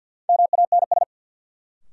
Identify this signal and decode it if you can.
Morse code, heard as a keyed tone, about 46 wpm, 700 Hz. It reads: MANS
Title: Sentences from Top 1000 Words - Review Part 6 - New Word Drill - 50wpm